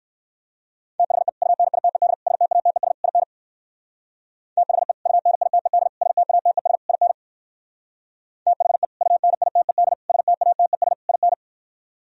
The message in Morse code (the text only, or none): THE UNITED STATES IN THE UNITED STATES IN THE UNITED STATES IN